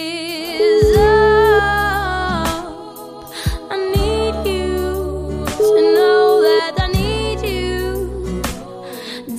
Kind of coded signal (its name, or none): telephone call progress tone